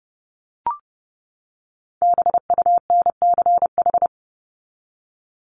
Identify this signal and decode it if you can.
Morse code, heard as a keyed tone, about 30 wpm, 700 Hz. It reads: BUNCH
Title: Top 1751-2000 Words 30wpm